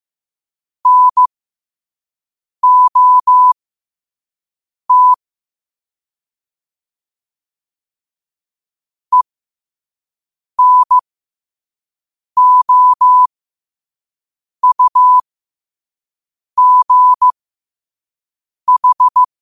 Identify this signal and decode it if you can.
Morse code, heard as a keyed tone, about 15 wpm, spaced as for 5 wpm, 1000 Hz. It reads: NOT ENOUGH
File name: qotdfiles / 5wpm